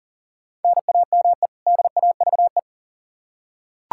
Morse code, heard as a keyed tone, about 40 wpm, 700 Hz. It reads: NAME DAVE
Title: Practice QSO - Exchange 1 - Beginning - 40wpm